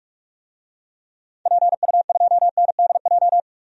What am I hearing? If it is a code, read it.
Morse code, heard as a keyed tone, about 46 wpm, 700 Hz. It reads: PU2NDJ
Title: DXCC-100 Call Signs - 6 Characters - 45wpm